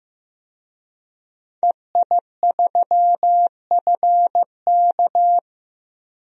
Morse code, heard as a keyed tone, about 15 wpm, 700 Hz. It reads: EI3FK
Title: DXCC-100 Call Signs - 5 Characters - 15wpm